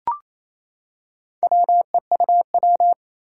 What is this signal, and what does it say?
Morse code, heard as a keyed tone, about 27 wpm, 700 Hz. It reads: WEUW
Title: Four Challenging Letter-Number Combinations 28wpm